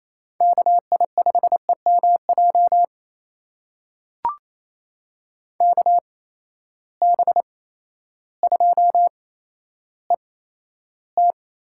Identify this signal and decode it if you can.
Morse code, heard as a keyed tone, about 28 wpm, spaced as for 13 wpm, 700 Hz. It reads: KI5EMJ KB2ET
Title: USA Call Signs - 6 Characters - 8x Character Spacing - 28wpm